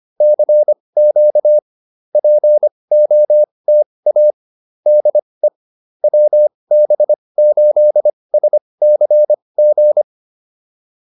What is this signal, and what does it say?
Morse code, heard as a keyed tone, about 25 wpm, spaced as for 20 wpm, 600 Hz. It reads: CQ POTA DE WB8SCG